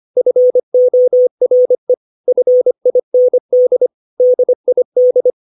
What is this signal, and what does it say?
Morse code, heard as a keyed tone, about 25 wpm, 500 Hz. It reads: FORE FIND DID
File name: 632 words / 632 Words 25WPM